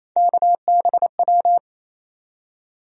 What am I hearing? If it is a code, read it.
Morse code, heard as a keyed tone, about 28 wpm, 700 Hz. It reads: KBW
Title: USA Call Sign Postfixes - 3 Letters 28wpm